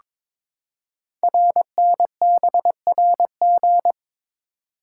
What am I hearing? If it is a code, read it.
Morse code, heard as a keyed tone, about 22 wpm, 700 Hz. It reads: RNBRG
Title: Code Groups - 5 Letters - 22wpm